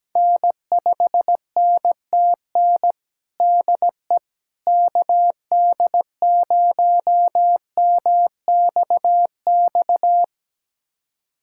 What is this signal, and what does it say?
Morse code, heard as a keyed tone, about 17 wpm, 700 Hz. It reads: N5NTN DE KD0MXX
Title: USA Callsign DE Callsign 17wpm